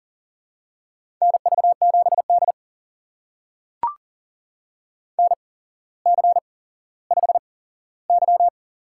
Morse code, heard as a keyed tone, about 39 wpm, spaced as for 17 wpm, 700 Hz. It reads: NV7D NC5Y
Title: USA Call Signs - 8x Character Spacing - 40wpm